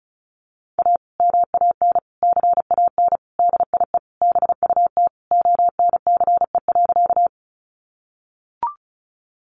Morse code, heard as a keyed tone, about 34 wpm, 700 Hz. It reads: A MAN CAN DIE BUT ONCE.
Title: Sentences from Top 1300 Words 35wpm